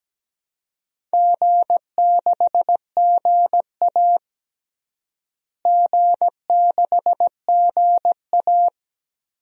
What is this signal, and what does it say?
Morse code, heard as a keyed tone, about 17 wpm, 700 Hz. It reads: G6GA G6GA
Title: DXCC-100 Call Signs - 4 Characters - Repeated 2x - 17wpm